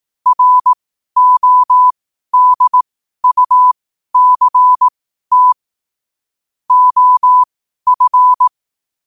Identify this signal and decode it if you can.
Morse code, heard as a keyed tone, about 18 wpm, spaced as for 12 wpm, 1000 Hz. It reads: RODUCT OF
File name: qotdfiles / 12wpm